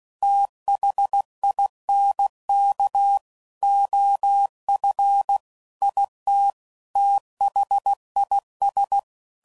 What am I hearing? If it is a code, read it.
Morse code, heard as a keyed tone, about 16 wpm, 800 Hz. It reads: THINK OF IT THIS
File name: CW Code Groups @15WPM / Technical Subjects / OR Gate